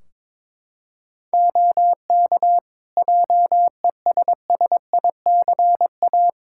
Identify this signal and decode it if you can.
Morse code, heard as a keyed tone, about 22 wpm, 700 Hz. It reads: OK JESSICA
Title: Practice QSO - Exchange 4 - Ending - 22wpm